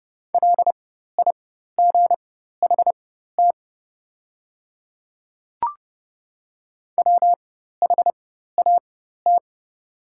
Morse code, heard as a keyed tone, about 29 wpm, spaced as for 14 wpm, 690 Hz. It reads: LIGHT WHAT
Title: 4x Character Spacing - Top 300 Words - 30wpm